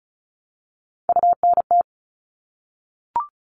Morse code, heard as a keyed tone, about 34 wpm, 700 Hz. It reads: UNT
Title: All Cut Numbers - 3-Digit 35wpm